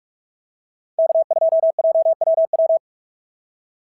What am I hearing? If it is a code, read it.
Morse code, heard as a keyed tone, about 43 wpm, 690 Hz. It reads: K2JWW
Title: SOTA QSO Practice 45wpm